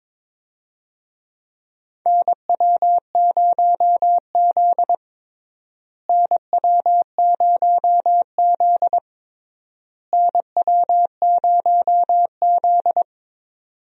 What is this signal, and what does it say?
Morse code, heard as a keyed tone, about 22 wpm, 700 Hz. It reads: NW0Z NW0Z NW0Z
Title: USA Call Signs - 4 Characters - Repeated 3x - 22wpm